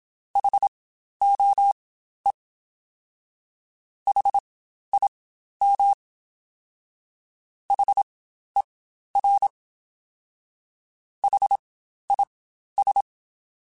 Morse code, heard as a keyed tone, about 27 wpm, spaced as for 12 wpm, 800 Hz. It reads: HOE HIM HER HIS
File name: Spaced Alphas